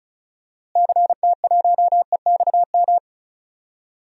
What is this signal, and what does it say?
Morse code, heard as a keyed tone, about 36 wpm, 700 Hz. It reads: CT1EXM
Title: DXCC-100 Call Signs - 6 Characters - Repeated 2x - 35wpm